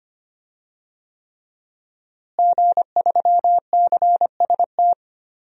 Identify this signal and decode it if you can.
Morse code, heard as a keyed tone, about 25 wpm, 700 Hz. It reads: G3CST
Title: DXCC-100 Call Signs - 5 Characters - Repeated 3x - 25wpm